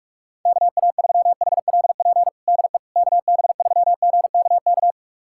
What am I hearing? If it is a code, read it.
Morse code, heard as a keyed tone, about 41 wpm, 700 Hz. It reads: KA3HLP DE KB3GKK
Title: USA Callsign DE Callsign 45wpm